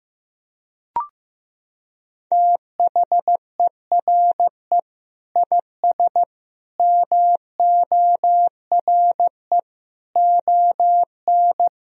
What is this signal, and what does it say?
Morse code, heard as a keyed tone, about 15 wpm, 700 Hz. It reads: THERE IS MORE ON
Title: Sentences from Top 1000 Words - Review Part 1 - 15wpm